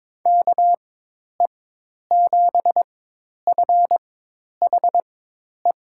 Morse code, heard as a keyed tone, about 22 wpm, spaced as for 10 wpm, 700 Hz. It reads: KE7FHE
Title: USA Call Signs - 6 Characters - 4x Character Spacing - 22wpm